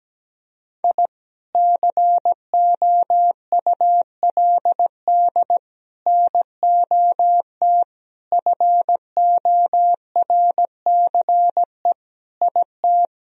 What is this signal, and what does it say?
Morse code, heard as a keyed tone, about 17 wpm, 700 Hz. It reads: I COULD NOT FORCE IT
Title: Sentences from Top 1000 Words - Review Part 3 - 17wpm